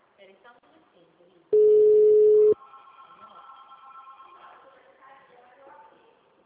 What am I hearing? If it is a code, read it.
Ringback tone (Europe, 425 Hz), with speech.